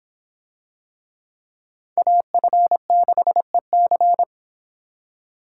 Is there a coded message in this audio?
Yes: Morse code from 2.0 to 4.2 s, heard as a keyed tone, about 25 wpm, 700 Hz. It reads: AF6EC